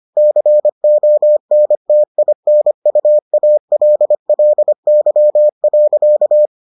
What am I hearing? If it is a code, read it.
Morse code, heard as a keyed tone, about 25 wpm, 600 Hz. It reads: CONTINUALLY.